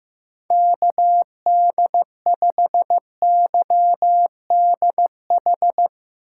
Morse code, heard as a keyed tone, about 15 wpm, 700 Hz. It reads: KD5YDH
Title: USA Call Signs - 6 Characters - 3x Character Spacing - 15wpm